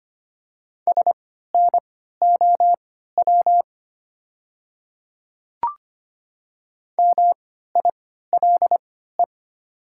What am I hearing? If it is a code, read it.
Morse code, heard as a keyed tone, about 25 wpm, spaced as for 14 wpm, 700 Hz. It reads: SNOW MILE
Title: 3x Character Spacing - Top 500 Words - 25wpm